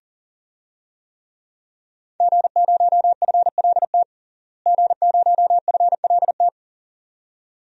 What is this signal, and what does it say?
Morse code, heard as a keyed tone, about 37 wpm, 700 Hz. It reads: G0FLT G0FLT
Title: DXCC-100 Call Signs - 5 Characters - Repeated 2x - 40wpm